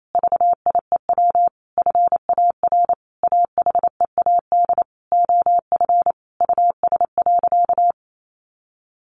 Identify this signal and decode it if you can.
Morse code, heard as a keyed tone, about 28 wpm, 700 Hz. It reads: VIEW FAR AHEAD OF US.
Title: Sentences from Top 1100 Words 28wpm